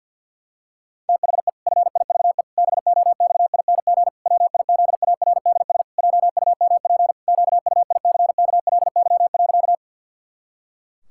Morse code, heard as a keyed tone, about 50 wpm, 700 Hz. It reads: THE FIVE BOXING WIZARDS JUMP QUICKLY.
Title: Pangram Challenge v2 - 50wpm